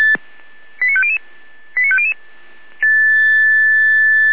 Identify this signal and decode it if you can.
Five-tone selective call (ZVEI), heard as quick runs of five tones, tones of about 80 ms. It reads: 89500 89500
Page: /455/SDR/ZVEI/88102/